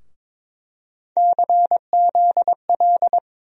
Morse code, heard as a keyed tone, about 22 wpm, 700 Hz. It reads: CZL
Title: USA Call Sign Postfixes - 3 Letters 22wpm